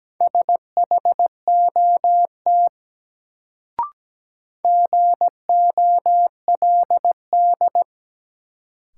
Morse code, heard as a keyed tone, about 17 wpm, 700 Hz. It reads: SHOT GOLD
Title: Sentences from Top 1100 Words - New Word Drill Alt - 17wpm